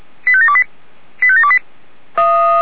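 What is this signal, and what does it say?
Five-tone selective call (ZVEI), heard as quick runs of five tones, tones of about 70 ms. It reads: 85728 85728
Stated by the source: ZVEI five-tone call 85727 received on an SDR.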